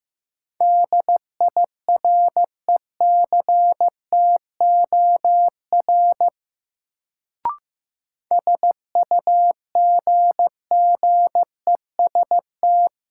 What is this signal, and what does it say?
Morse code, heard as a keyed tone, about 15 wpm, 700 Hz. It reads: DIRECTOR SUGGEST
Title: Sentences from Top 1300 Words - New Word Drill - 15wpm